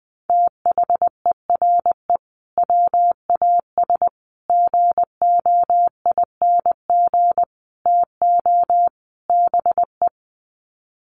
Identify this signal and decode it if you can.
Morse code, heard as a keyed tone, about 20 wpm, 700 Hz. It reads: THERE WAS GOING TO BE
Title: Sets of 5 Words Encore2 20wpm